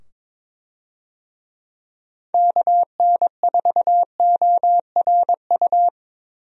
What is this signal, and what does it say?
Morse code, heard as a keyed tone, about 22 wpm, 700 Hz. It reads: KN4ORU
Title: USA Call Signs - 6 Characters - 22wpm